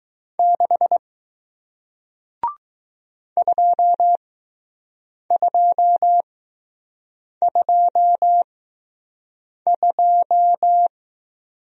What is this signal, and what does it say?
Morse code, heard as a keyed tone, about 20 wpm, spaced as for 6 wpm, 700 Hz. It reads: 6 2222